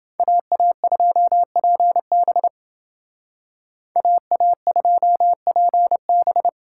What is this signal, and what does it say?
Morse code, heard as a keyed tone, about 29 wpm, 700 Hz. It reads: AA2PB AA2PB